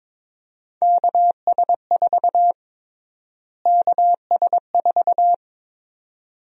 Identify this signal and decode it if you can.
Morse code, heard as a keyed tone, about 22 wpm, 700 Hz. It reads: KS4 KS4
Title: USA Call Sign Prefixes - Repeated 2x - 22wpm